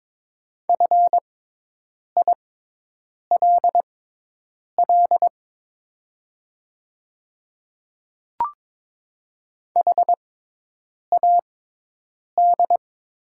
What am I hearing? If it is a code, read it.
Morse code, heard as a keyed tone, about 22 wpm, spaced as for 8 wpm, 700 Hz. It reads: FILL HAD
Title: 6x Character Spacing - Top 500 Words - 22wpm